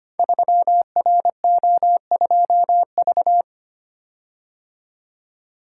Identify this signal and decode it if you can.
Morse code, heard as a keyed tone, about 25 wpm, 700 Hz. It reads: 3RO2V
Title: Code Groups - 5 Letter:Numbers - 25wpm